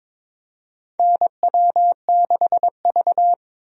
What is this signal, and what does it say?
Morse code, heard as a keyed tone, about 22 wpm, 700 Hz. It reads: NW6V
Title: USA Call Signs - 4 Characters - Repeated 3x - 22wpm